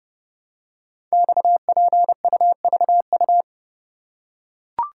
Morse code, heard as a keyed tone, about 30 wpm, 700 Hz. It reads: XPUVU
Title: Code Groups - 5 Letters - 30wpm